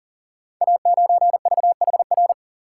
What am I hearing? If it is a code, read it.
Morse code, heard as a keyed tone, about 38 wpm, 690 Hz. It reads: A9VHR
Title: Code Groups - 5 Letter:Numbers - 40wpm